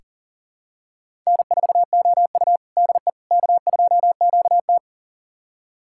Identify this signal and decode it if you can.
Morse code, heard as a keyed tone, about 40 wpm, 700 Hz. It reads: N4OU DE K2QT
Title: USA Callsign DE Callsign - 4 Characters - 40wpm